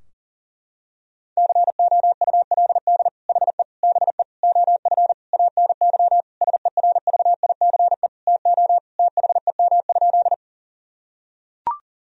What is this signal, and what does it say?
Morse code, heard as a keyed tone, about 40 wpm, 700 Hz. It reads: COULD HE BE OF ANY SERVICE TO THEM?
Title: Sentences from Top 1000 Words - Review Part 4 - 40wpm